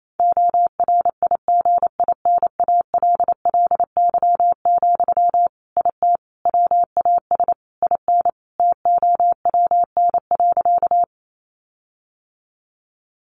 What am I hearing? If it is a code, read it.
Morse code, heard as a keyed tone, about 28 wpm, 700 Hz. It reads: ORIGINALLY, IT WAS IN TOWN.
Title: Sentences from Top 1500 Words 28wpm